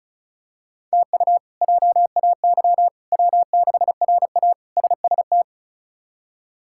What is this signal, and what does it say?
Morse code, heard as a keyed tone, about 34 wpm, 700 Hz. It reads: TU JAY W6RA SST
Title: K1USN SST QSO Practice - 35wpm